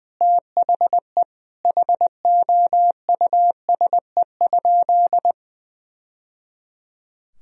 Morse code, heard as a keyed tone, about 21 wpm, 700 Hz. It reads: THE HOUSE?
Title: Sentences from Top 1000 Words - Review Part 2 - 20wpm